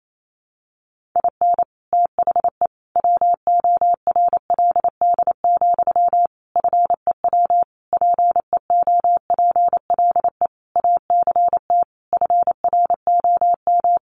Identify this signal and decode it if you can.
Morse code, heard as a keyed tone, about 28 wpm, 700 Hz. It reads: IN THE WORLD, FEW PEOPLE ACT FROM
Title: Sentences from Top 1400 Words 28wpm